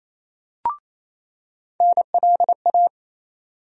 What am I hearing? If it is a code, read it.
Morse code, heard as a keyed tone, about 28 wpm, 700 Hz. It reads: NLA
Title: USA Call Sign Postfixes - 3 Letters 28wpm